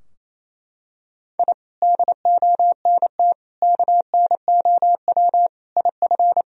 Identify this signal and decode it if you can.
Morse code, heard as a keyed tone, about 27 wpm, 700 Hz. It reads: I DONT KNOW IF
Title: Sentences from Top 1000 Words - Review Part 2 - 28wpm